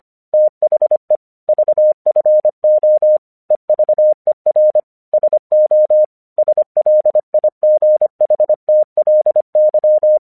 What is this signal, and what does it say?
Morse code, heard as a keyed tone, about 24 wpm, 610 Hz. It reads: THE VFO EVER SO SLIGHTLY
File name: 1 English 25wpm / AQC28